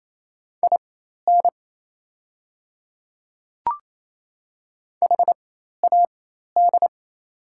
Morse code, heard as a keyed tone, about 28 wpm, spaced as for 13 wpm, 700 Hz. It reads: IN HAD
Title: 4x Character Spacing - Top 200 Words - 28wpm